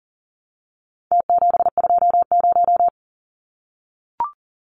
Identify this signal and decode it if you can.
Morse code, heard as a keyed tone, about 40 wpm, 700 Hz. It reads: T720
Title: Numbers - Contest Style - 4-Digit 40wpm